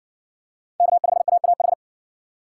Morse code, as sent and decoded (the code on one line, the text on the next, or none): -.- ..... .- .- ....
K5AAH